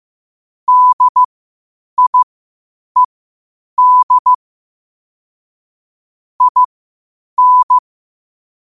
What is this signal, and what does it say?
Morse code, heard as a keyed tone, about 15 wpm, spaced as for 8 wpm, 1000 Hz. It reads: DIED IN
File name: qotdfiles / 8wpm